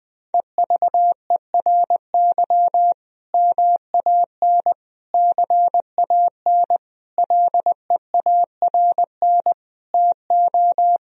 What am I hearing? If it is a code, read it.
Morse code, heard as a keyed tone, about 20 wpm, 700 Hz. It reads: EVERY MAN CAN LEARN TO
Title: Sentences from Top 1000 Words - Review Part 1 - 20wpm